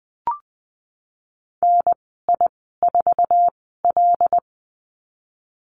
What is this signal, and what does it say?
Morse code, heard as a keyed tone, about 19 wpm, spaced as for 14 wpm, 700 Hz. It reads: NI4L
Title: 076 Lesson 13g - Letter L - Call Signs 20wpm